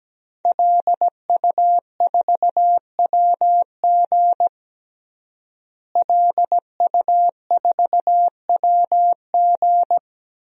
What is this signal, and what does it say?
Morse code, heard as a keyed tone, about 17 wpm, 700 Hz. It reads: LU4WG LU4WG
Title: DXCC-100 Call Signs - 5 Characters - Repeated 3x - 17wpm